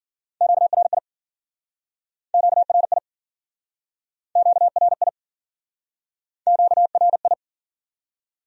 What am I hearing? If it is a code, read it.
Morse code, heard as a keyed tone, about 53 wpm, 700 Hz. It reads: QRI QRI QRI QRI